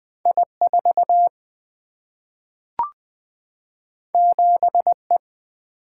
Morse code, heard as a keyed tone, about 20 wpm, 700 Hz. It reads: I4 7E
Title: Single-Letter Single-Number 20wpm